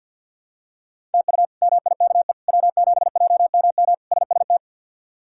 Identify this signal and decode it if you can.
Morse code, heard as a keyed tone, about 47 wpm, 700 Hz. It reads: TU MIKE W7JMM SST
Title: K1USN SST QSO Practice - 50wpm